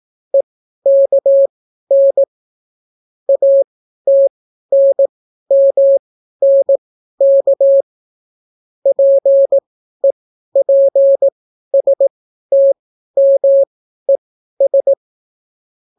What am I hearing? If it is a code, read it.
Morse code, heard as a keyed tone, about 18 wpm, spaced as for 12 wpm, 550 Hz. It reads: EKN ATNMNK PEPSTMES